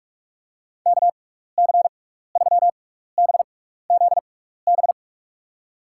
Morse code, heard as a keyed tone, about 46 wpm, spaced as for 15 wpm, 700 Hz. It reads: KC3BZB
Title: USA Call Signs - 6x Character Spacing - 45wpm